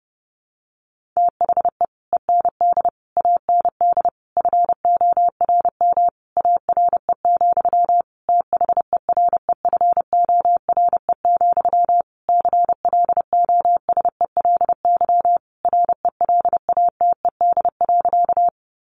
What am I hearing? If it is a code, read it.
Morse code, heard as a keyed tone, about 30 wpm, 700 Hz. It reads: THE END AND FORM ARE, THEREFORE, CLOSELY RELATED.
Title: Sentences from Top 1400 Words 30wpm